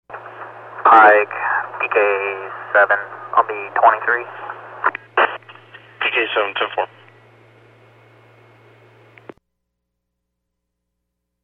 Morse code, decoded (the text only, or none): none